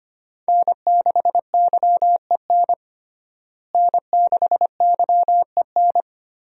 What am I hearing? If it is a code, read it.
Morse code, heard as a keyed tone, about 25 wpm, 700 Hz. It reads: N6YEN N6YEN
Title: USA Call Signs - 5 Characters - Repeated 3x - 25wpm